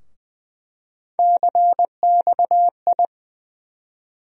Morse code, heard as a keyed tone, about 20 wpm, 700 Hz. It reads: CXI
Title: USA Call Sign Postfixes - 3 Letters 20wpm